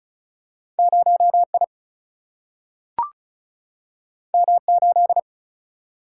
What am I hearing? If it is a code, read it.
Morse code, heard as a keyed tone, about 36 wpm, 700 Hz. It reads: 0 I M 8
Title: Single-Letter Single-Number 35wpm